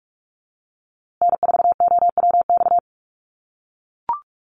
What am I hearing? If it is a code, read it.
Morse code, heard as a keyed tone, about 44 wpm, 700 Hz. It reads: N4OWX
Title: USA Call Signs Repeated 3x 45wpm